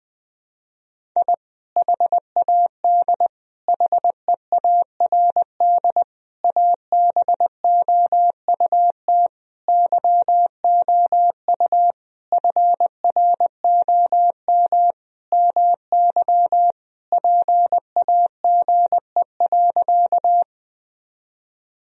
Morse code, heard as a keyed tone, about 20 wpm, 700 Hz. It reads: I HAD HEARD ABOUT YOU FROM MY PAGE.
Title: Sentences from Top 1000 Words - Review Part 1 - 20wpm